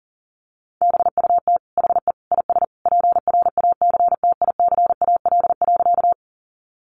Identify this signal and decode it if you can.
Morse code, heard as a keyed tone, about 37 wpm, 700 Hz. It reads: BUT HE IS PRACTICAL.
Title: Sentences from Top 1400 Words 40wpm